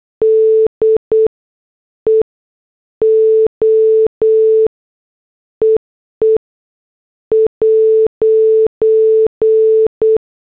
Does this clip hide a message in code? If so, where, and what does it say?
Yes: Morse code from 0.2 to 10.2 s, heard as a keyed tone, about 8 wpm, spaced as for 7 wpm, 440 Hz. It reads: DEO EE '